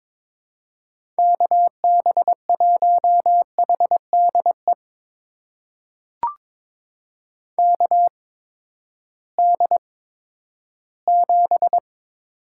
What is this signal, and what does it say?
Morse code, heard as a keyed tone, about 22 wpm, spaced as for 13 wpm, 700 Hz. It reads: KB1HDE KD7
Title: USA Call Signs - 6 Characters - 8x Character Spacing - 22wpm